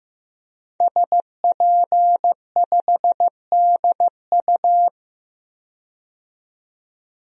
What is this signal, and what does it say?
Morse code, heard as a keyed tone, about 15 wpm, 700 Hz. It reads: SP5DU
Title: DXCC-100 Call Signs - 5 Characters - 15wpm